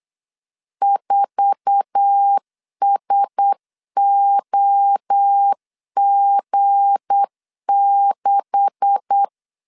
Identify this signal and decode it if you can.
Morse code, heard as a keyed tone, about 8 wpm, 800 Hz. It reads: 4SOG6
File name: CW / DET / 1 9 1500C S C 8wpm 5653